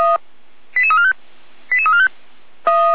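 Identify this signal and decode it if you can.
Five-tone selective call (ZVEI), heard as quick runs of five tones, tones of about 70 ms. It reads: 88345 88345
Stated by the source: ZVEI five-tone selective call 88211 received on SDR.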